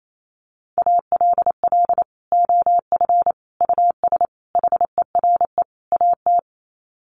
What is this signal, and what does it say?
Morse code, heard as a keyed tone, about 28 wpm, 700 Hz. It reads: ALL OF US HERE AT
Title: Sets of 5 Words 28wpm